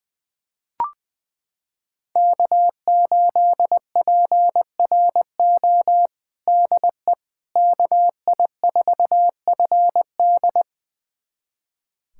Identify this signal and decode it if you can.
Morse code, heard as a keyed tone, about 21 wpm, 700 Hz. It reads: K8PRO DE KI4FD
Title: USA Callsign DE Callsign - 5 Characters - 20wpm